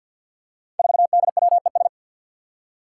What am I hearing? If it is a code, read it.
Morse code, heard as a keyed tone, about 52 wpm, 700 Hz. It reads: 4DWES
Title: Five Challenging Letter-Number Combinations 50wpm